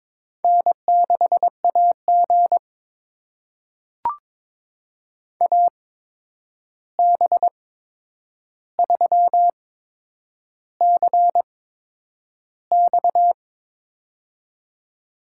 Morse code, heard as a keyed tone, about 22 wpm, spaced as for 9 wpm, 700 Hz. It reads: N6AG AB3CX
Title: USA Call Signs - 8x Character Spacing - 22wpm